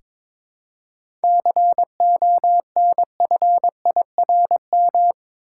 Morse code, heard as a keyed tone, about 21 wpm, 700 Hz. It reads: CONFIRM
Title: Seven Letter Words - 22wpm